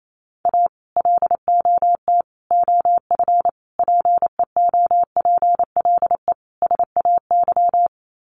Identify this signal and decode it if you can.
Morse code, heard as a keyed tone, about 28 wpm, 700 Hz. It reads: A LOT OF PEOPLE SAY
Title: Sets of 5 Words Encore2 28wpm